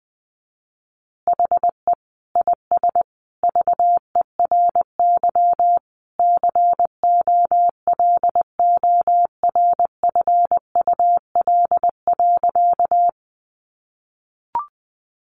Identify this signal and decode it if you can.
Morse code, heard as a keyed tone, about 20 wpm, 700 Hz. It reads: HE IS VERY COLORFUL.
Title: Easy Phrases 20wpm